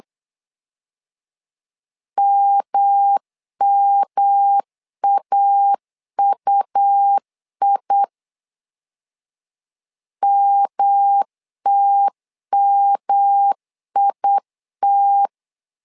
Morse code, heard as a keyed tone, about 8 wpm, 800 Hz. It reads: MMAUI MTMIT
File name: LEZIONI / 3 MWRO / 3 M 8wpm 2300